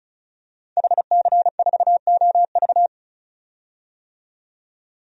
Morse code, heard as a keyed tone, about 36 wpm, 700 Hz. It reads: HC4OV
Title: DXCC-100 Call Signs - 5 Characters - 35wpm